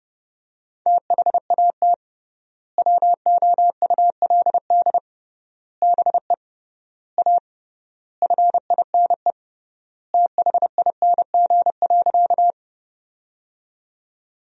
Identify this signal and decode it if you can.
Morse code, heard as a keyed tone, about 30 wpm, 700 Hz. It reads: THAT WOULD BE A FINE THING.
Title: 3x Word-Spacing - Sentences from Top 500 Words - 30wpm